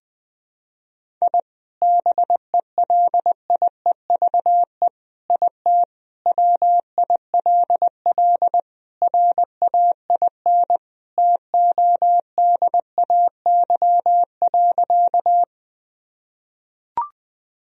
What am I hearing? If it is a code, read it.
Morse code, heard as a keyed tone, about 21 wpm, 700 Hz. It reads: I BELIEVE IT WILL RAIN TODAY.
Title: Sentences from Top 1000 Words - Review Part 1 - 20wpm